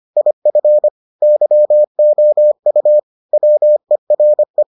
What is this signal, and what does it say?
Morse code, heard as a keyed tone, about 25 wpm, 600 Hz. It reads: IF YOU WERE